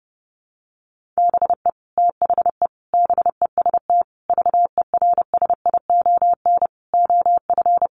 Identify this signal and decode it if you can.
Morse code, heard as a keyed tone, about 30 wpm, 700 Hz. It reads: BE THE BEST VERSION OF
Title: Sentences from Top 1400 Words 30wpm